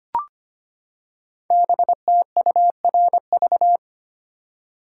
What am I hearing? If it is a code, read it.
Morse code, heard as a keyed tone, about 25 wpm, 700 Hz. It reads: BTURV
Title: Code Groups - 5 Letters - 25wpm